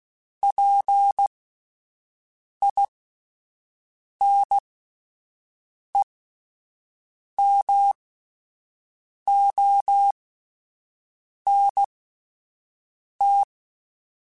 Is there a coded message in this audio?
Yes: Morse code from 0.4 to 13.4 s, heard as a keyed tone, about 16 wpm, spaced as for 5 wpm, 800 Hz. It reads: PINEMONT